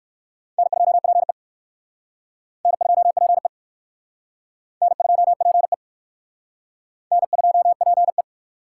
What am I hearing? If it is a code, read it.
Morse code, heard as a keyed tone, about 58 wpm, 700 Hz. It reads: N2PE N2PE N2PE N2PE